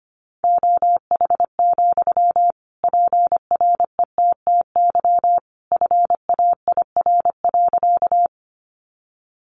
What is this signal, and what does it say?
Morse code, heard as a keyed tone, about 25 wpm, 700 Hz. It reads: OH, PRETTY FAIR.
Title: Sentences from Top 1300 Words 25wpm